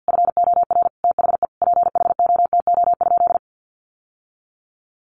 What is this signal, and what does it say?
Morse code, heard as a keyed tone, about 50 wpm, 700 Hz. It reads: FOR THE PHOTO?
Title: Sentences from Top 1200 Words 50wpm